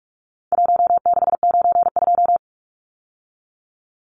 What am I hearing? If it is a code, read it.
Morse code, heard as a keyed tone, about 44 wpm, 700 Hz. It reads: 1692
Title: Numbers - 4-Digit 45wpm